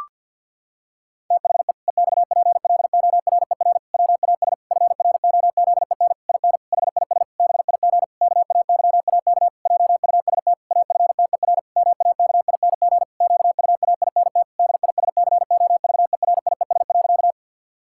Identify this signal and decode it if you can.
Morse code, heard as a keyed tone, about 50 wpm, 700 Hz. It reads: THE EXPLORER WAS FROZEN IN HIS BIG KAYAK JUST AFTER MAKING QUAINT DISCOVERIES.
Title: Pangram Challenge v2 - 50wpm